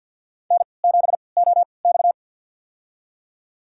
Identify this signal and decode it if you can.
Morse code, heard as a keyed tone, about 52 wpm, spaced as for 29 wpm, 700 Hz. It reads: N7OX
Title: USA Call Signs - 4 Characters - 3x Character Spacing - 50wpm